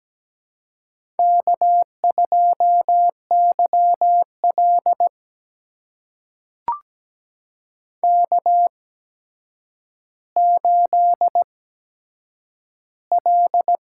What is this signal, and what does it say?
Morse code, heard as a keyed tone, about 17 wpm, spaced as for 8 wpm, 700 Hz. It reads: K2YL K8L
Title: USA Call Signs - 4 Characters - 8x Character Spacing - 17wpm